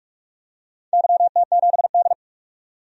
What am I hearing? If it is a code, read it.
Morse code, heard as a keyed tone, about 46 wpm, 700 Hz. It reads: YT7D
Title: DXCC-100 Call Signs - 4 Characters - 45wpm